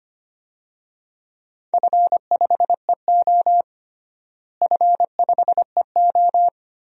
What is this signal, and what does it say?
Morse code, heard as a keyed tone, about 25 wpm, 700 Hz. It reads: F5EO F5EO
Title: DXCC-100 Call Signs - 4 Characters - Repeated 3x - 25wpm